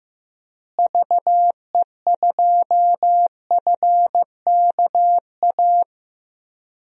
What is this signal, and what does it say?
Morse code, heard as a keyed tone, about 15 wpm, 700 Hz. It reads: VE2FKA
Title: DXCC-100 Call Signs - 6 Characters - Repeated 3x - 15wpm